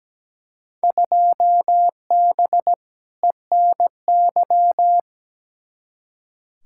Morse code, heard as a keyed tone, about 17 wpm, 700 Hz. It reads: 2B ENY
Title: Field Day Exchanges Encore - 17wpm